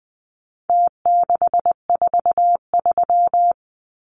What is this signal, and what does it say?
Morse code, heard as a keyed tone, about 20 wpm, 700 Hz. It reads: T643
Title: Numbers - Contest Style - 4-Digit 20wpm